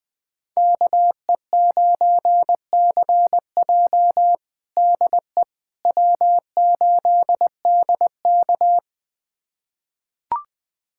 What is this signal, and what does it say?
Morse code, heard as a keyed tone, about 20 wpm, 700 Hz. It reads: KE9CJ DE W8DK
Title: USA Callsign DE Callsign 20wpm